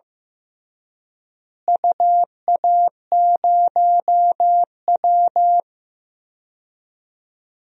Morse code, heard as a keyed tone, about 15 wpm, 700 Hz. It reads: UA0W
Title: DXCC-100 Call Signs - 4 Characters - Repeated 3x - 15wpm